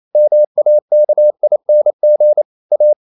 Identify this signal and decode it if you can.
Morse code, heard as a keyed tone, about 28 wpm, 600 Hz. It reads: MAKING A
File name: CW / Practice / EnglishQuotations0128